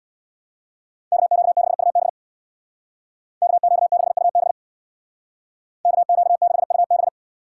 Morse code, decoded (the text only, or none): KQ6UB KQ6UB KQ6UB